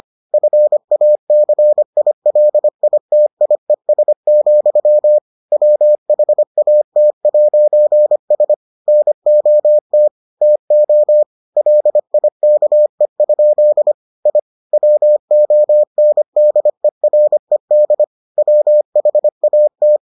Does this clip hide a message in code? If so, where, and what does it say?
Yes: Morse code from 0.3 to 20.0 s, heard as a keyed tone, about 25 wpm, 600 Hz. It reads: FACILITIES, WHAT'S NOT TO LIKE? I WONDERED WHAT